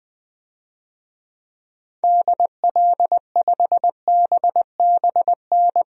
Morse code, heard as a keyed tone, about 20 wpm, 700 Hz. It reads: DL5BBN
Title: DXCC-100 Call Signs - 6 Characters - 20wpm